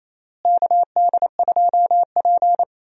Morse code, heard as a keyed tone, about 28 wpm, 700 Hz. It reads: KD2P
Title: USA Call Signs - 4 Characters - Repeated 3x - 28wpm